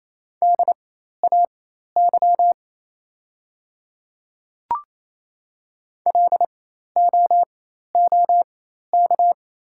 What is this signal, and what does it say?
Morse code, heard as a keyed tone, about 28 wpm, spaced as for 13 wpm, 700 Hz. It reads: DAY LOOK
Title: 4x Character Spacing - Top 100 Words - 28wpm